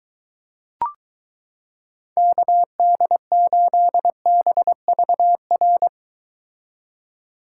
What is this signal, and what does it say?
Morse code, heard as a keyed tone, about 22 wpm, 700 Hz. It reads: KD8BVR